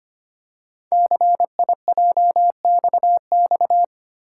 Morse code, heard as a keyed tone, about 25 wpm, 700 Hz. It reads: CIJXX
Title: Code Groups - 5 Letters - 25wpm